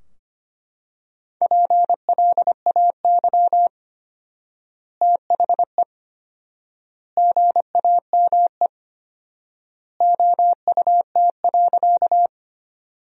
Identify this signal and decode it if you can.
Morse code, heard as a keyed tone, about 25 wpm, 700 Hz. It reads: PLAY THE GAME OUT.
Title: 4x Word-Spacing - Sentences from Top 500 Words - 25wpm